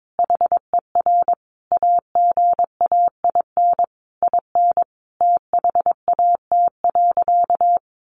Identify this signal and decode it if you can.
Morse code, heard as a keyed tone, about 22 wpm, 700 Hz. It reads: HER AGAIN IN THAT.
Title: Sentences from Top 200 Words 22wpm